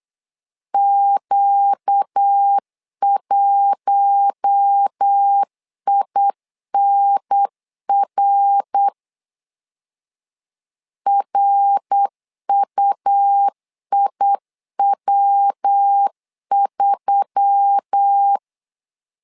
Morse code, heard as a keyed tone, about 8 wpm, 800 Hz. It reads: Q1INR RUIW3
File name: LEZIONI / 4 Q1S3 8wpm 3037